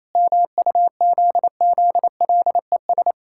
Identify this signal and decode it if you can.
Morse code, heard as a keyed tone, about 27 wpm, 700 Hz. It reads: MUZZLES
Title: Pangram Challenge v2 - 28wpm